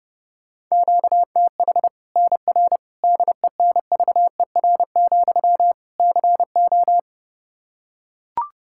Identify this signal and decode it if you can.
Morse code, heard as a keyed tone, about 29 wpm, 700 Hz. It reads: QTH NR DENVER, CO
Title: K1USN SST QSO Practice 30wpm